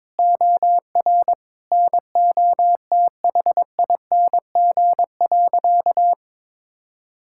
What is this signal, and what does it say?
Morse code, heard as a keyed tone, about 22 wpm, 700 Hz. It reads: OR NOTHING.
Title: Sentences from Top 1000 Words - Review Part 6 - 22wpm